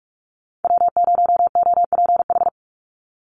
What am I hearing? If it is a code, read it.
Morse code, heard as a keyed tone, about 44 wpm, 700 Hz. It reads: W0OPH
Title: USA Call Signs 45wpm